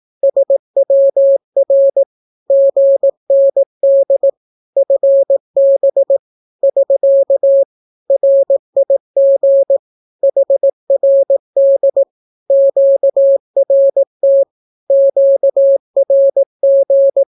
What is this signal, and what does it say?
Morse code, heard as a keyed tone, about 18 wpm, 550 Hz. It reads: SWR GND FB <SK> RIG HRD QRT QRG
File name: naukapodstaw / skroty / skroty